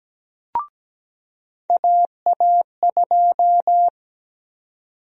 Morse code, heard as a keyed tone, about 17 wpm, 700 Hz. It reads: AA2
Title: USA Call Sign Prefixes - 17wpm